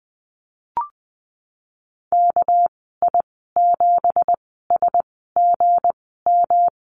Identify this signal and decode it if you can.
Morse code, heard as a keyed tone, about 20 wpm, spaced as for 15 wpm, 700 Hz. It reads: KI7SGM
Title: 207 Lesson 33g - Letter j - Call Signs 20wpm